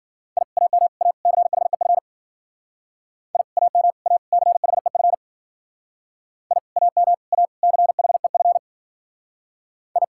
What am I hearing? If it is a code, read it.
Morse code, heard as a keyed tone, about 53 wpm, 700 Hz. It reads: I AM A CHEF I AM A CHEF I AM A CHEF I